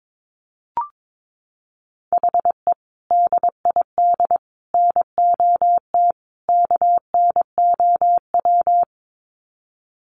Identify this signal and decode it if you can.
Morse code, heard as a keyed tone, about 22 wpm, 700 Hz. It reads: HE DID NOT KNOW
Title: Sets of 4 Words 22wpm